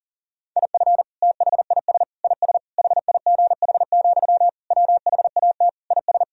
Morse code, heard as a keyed tone, about 40 wpm, 700 Hz. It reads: IF THIS IS HIGH, WHAT IS
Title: Sentences from Top 1000 Words - Review Part 1 - 40wpm